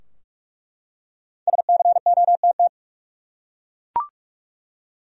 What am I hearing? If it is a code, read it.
Morse code, heard as a keyed tone, about 46 wpm, 700 Hz. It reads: SCOTT
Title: USA Common Names - Max 5 Letters - 45wpm